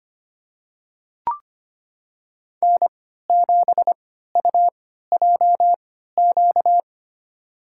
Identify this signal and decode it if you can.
Morse code, heard as a keyed tone, about 25 wpm, spaced as for 14 wpm, 700 Hz. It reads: N7UJQ
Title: USA Call Signs - 5 Characters - 3x Character Spacing - 25wpm